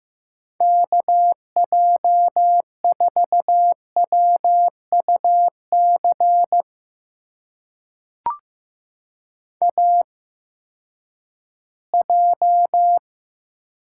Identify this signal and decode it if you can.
Morse code, heard as a keyed tone, about 15 wpm, spaced as for 10 wpm, 700 Hz. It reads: KJ4WUC AJ
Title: USA Call Signs - 8x Character Spacing - 15wpm